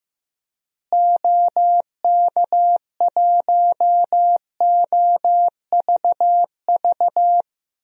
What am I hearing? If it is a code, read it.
Morse code, heard as a keyed tone, about 15 wpm, 700 Hz. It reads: OK1OVV
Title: DXCC-100 Call Signs - 6 Characters - Repeated 2x - 15wpm